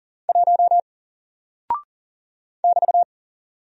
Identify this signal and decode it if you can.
Morse code, heard as a keyed tone, about 40 wpm, 700 Hz. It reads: 1 <BT>
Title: Single Character (Letter:Number:Punctuation:Prosign) 40wpm